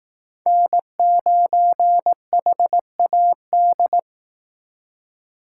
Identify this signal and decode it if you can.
Morse code, heard as a keyed tone, about 18 wpm, 700 Hz. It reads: N9HAD
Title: Speed Racing - USA 1x3 Call Signs 15wpm